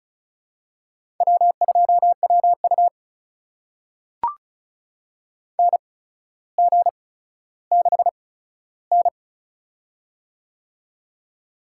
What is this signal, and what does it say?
Morse code, heard as a keyed tone, about 36 wpm, spaced as for 15 wpm, 700 Hz. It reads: W2WU NG6N
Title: USA Call Signs - 8x Character Spacing - 35wpm